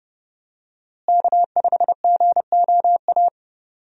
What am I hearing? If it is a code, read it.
Morse code, heard as a keyed tone, about 30 wpm, 700 Hz. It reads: K5GOA
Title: USA Call Signs - 5 Characters - Repeated 3x - 30wpm